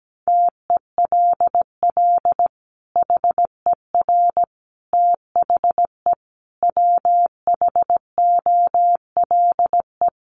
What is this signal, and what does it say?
Morse code, heard as a keyed tone, about 17 wpm, 700 Hz. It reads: TELL HER THE WHOLE
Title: Sentences from Top 800 Words 17wpm